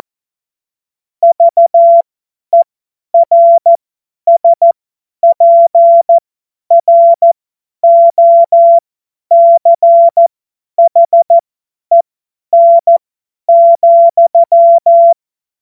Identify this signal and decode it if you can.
Morse code, heard as a keyed tone, about 14 wpm, spaced as for 10 wpm, 680 Hz. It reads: VERSPROCHEN,
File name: Morsekurs / Witze / Witz 13. Dezember 2022 50 F70